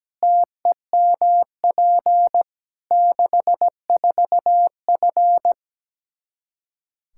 Morse code, heard as a keyed tone, about 17 wpm, 700 Hz. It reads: TEMP 64F
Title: K1USN SST QSO Practice 17wpm